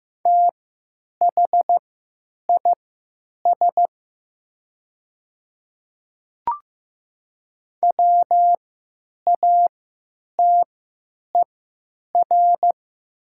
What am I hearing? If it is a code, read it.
Morse code, heard as a keyed tone, about 15 wpm, spaced as for 9 wpm, 700 Hz. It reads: THIS WATER
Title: 3x Character Spacing - Top 100 Words - 15wpm